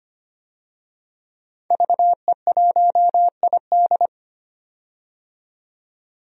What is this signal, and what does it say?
Morse code, heard as a keyed tone, about 25 wpm, 700 Hz. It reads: VE1ID
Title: DXCC-100 Call Signs - 5 Characters - 25wpm